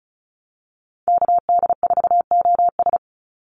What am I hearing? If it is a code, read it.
Morse code, heard as a keyed tone, about 34 wpm, 700 Hz. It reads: KD4OS
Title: Call Signs - 35wpm